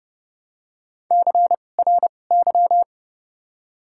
Morse code, heard as a keyed tone, about 30 wpm, 700 Hz. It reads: C R Y
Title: 2x Character Spacing - Top 500 Words - 30wpm